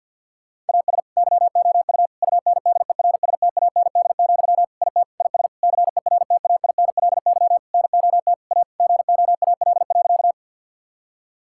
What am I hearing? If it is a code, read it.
Morse code, heard as a keyed tone, about 50 wpm, 700 Hz. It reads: AS YOU UNDERSTAND, IT IS CERTAINLY NOT A GOAL.
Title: Sentences from Top 1000 Words - Review Part 5 - 50wpm